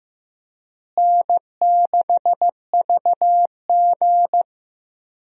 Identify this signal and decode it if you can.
Morse code, heard as a keyed tone, about 16 wpm, 700 Hz. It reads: N6VG
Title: USA Call Signs - 4 Characters - Repeated 3x - 15wpm